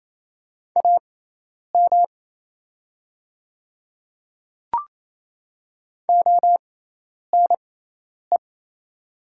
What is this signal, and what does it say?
Morse code, heard as a keyed tone, about 28 wpm, spaced as for 10 wpm, 700 Hz. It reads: AM ONE
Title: 6x Character Spacing - Top 500 Words - 28wpm